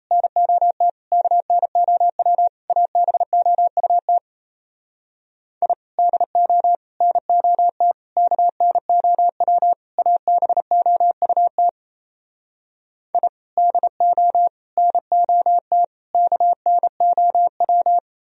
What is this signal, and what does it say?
Morse code, heard as a keyed tone, about 37 wpm, 700 Hz. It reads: NOT KNOW ABOUT I DO NOT KNOW ABOUT I DO NOT KNOW